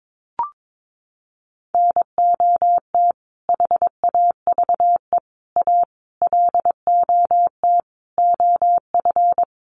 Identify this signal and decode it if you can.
Morse code, heard as a keyed tone, about 22 wpm, 700 Hz. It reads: NOT HAVE A LOT OF
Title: Sets of 5 Words 22wpm